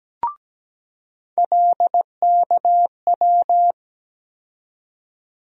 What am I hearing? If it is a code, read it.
Morse code, heard as a keyed tone, about 17 wpm, 700 Hz. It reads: LKW
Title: USA Call Sign Postfixes - 3 Letters 17wpm